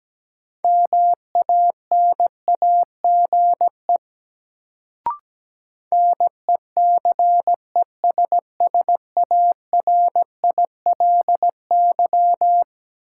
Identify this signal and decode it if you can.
Morse code, heard as a keyed tone, about 17 wpm, 700 Hz. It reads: MANAGE NECESSARILY
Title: Sentences from Top 1400 Words - New Word Drill Alt - 17wpm